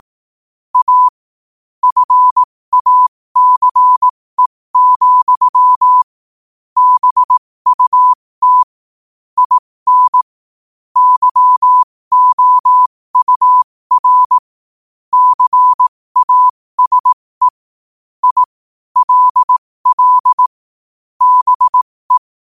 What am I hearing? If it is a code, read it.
Morse code, heard as a keyed tone, about 18 wpm, spaced as for 15 wpm, 1000 Hz. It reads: A FACE, BUT IN YOUR CASE I LL BE
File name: qotdfiles / 15wpm